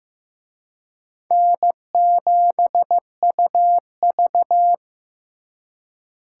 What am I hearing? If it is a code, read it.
Morse code, heard as a keyed tone, about 18 wpm, 700 Hz. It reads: N7UV
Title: Speed Racing - USA 1x2 Call Signs 15wpm